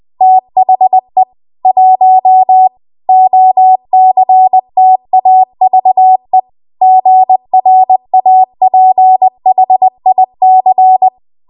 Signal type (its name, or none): Morse code